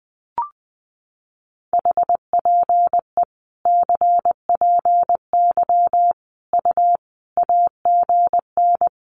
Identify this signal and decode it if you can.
Morse code, heard as a keyed tone, about 20 wpm, 700 Hz. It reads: HPE CPY U AGN
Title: Common QSO Phrases 20wpm